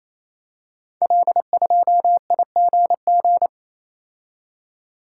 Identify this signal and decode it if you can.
Morse code, heard as a keyed tone, about 29 wpm, 700 Hz. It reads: L2IGG
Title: Code Groups - 5 Letter:Numbers - 28wpm